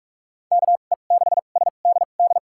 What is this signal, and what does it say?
Morse code, heard as a keyed tone, about 46 wpm, spaced as for 33 wpm, 700 Hz. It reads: KE6SDD
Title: USA Call Signs - 6 Characters - 2x Character Spacing - 45wpm